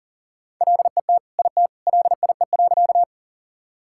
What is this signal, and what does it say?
Morse code, heard as a keyed tone, about 40 wpm, 700 Hz. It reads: LET IT LIE.
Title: Sentences from Top 1600 Words 40wpm